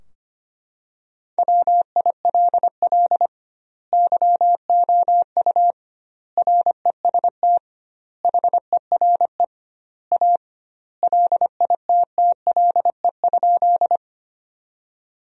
Morse code, heard as a keyed tone, about 25 wpm, 700 Hz. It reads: WILL YOU REST HERE A LITTLE?
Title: 2x Word-Spacing - Sentences from Top 500 Words - 25wpm